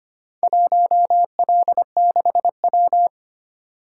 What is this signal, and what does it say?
Morse code, heard as a keyed tone, about 25 wpm, 700 Hz. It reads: 1L6W
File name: Four Letter-Number Combinations - 25wpm